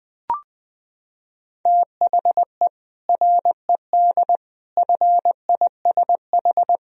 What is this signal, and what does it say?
Morse code, heard as a keyed tone, about 21 wpm, 700 Hz. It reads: THE RED FISH
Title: Sentences from Top 1000 Words - Review Part 2 - 20wpm